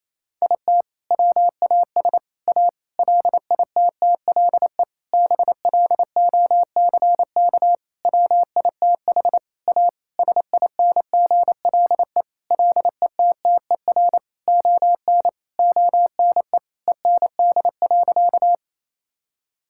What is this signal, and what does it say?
Morse code, heard as a keyed tone, about 28 wpm, 700 Hz. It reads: IT WAS A LITTLE BLOCK WITH A SINGLE LETTER ON ONE END.
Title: Sentences from Top 1600 Words 28wpm